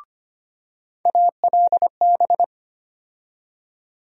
Morse code, heard as a keyed tone, about 25 wpm, 700 Hz. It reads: ALB
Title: USA Call Sign Postfixes - 3 Letters 25wpm